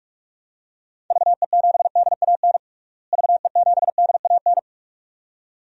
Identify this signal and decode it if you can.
Morse code, heard as a keyed tone, about 46 wpm, 700 Hz. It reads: VE7DAN VE7DAN
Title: DXCC-100 Call Signs - 6 Characters - Repeated 2x - 45wpm